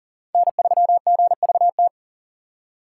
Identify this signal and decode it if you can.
Morse code, heard as a keyed tone, about 40 wpm, 700 Hz. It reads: N3GVT
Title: USA Call Signs - 5 Characters - 4x Character Spacing - 40wpm